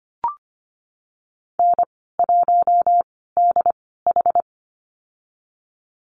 Morse code, heard as a keyed tone, about 25 wpm, spaced as for 16 wpm, 700 Hz. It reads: N1DH
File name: Lesson 15g - Number 5 - Call Signs 25wpm